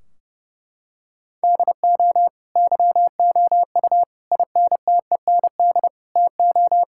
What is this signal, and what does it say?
Morse code, heard as a keyed tone, about 32 wpm, 700 Hz. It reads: DO YOU INTEND TO
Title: Sentences from Top 1600 Words 30wpm